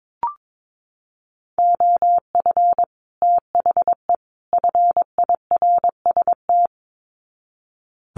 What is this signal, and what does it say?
Morse code, heard as a keyed tone, about 22 wpm, 700 Hz. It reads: OF THE FIRST
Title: Sets of 3 Words 22wpm